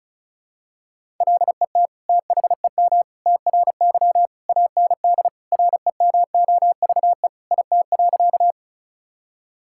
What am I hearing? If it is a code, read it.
Morse code, heard as a keyed tone, about 36 wpm, 690 Hz. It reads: LET THEM TRY AND REMOVE IT.
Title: Sentences from Top 1000 Words - Review Part 5 - 35wpm